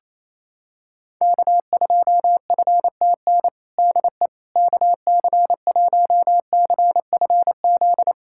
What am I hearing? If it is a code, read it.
Morse code, heard as a keyed tone, about 28 wpm, 700 Hz. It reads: K2FTN DE KC1CFZ
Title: USA Callsign DE Callsign 28wpm